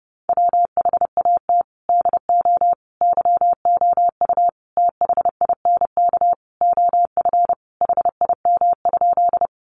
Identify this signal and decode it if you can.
Morse code, heard as a keyed tone, about 30 wpm, 700 Hz. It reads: WHAT DO YOU THINK OF HIM?
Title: Sentences from Top 200 Words 30wpm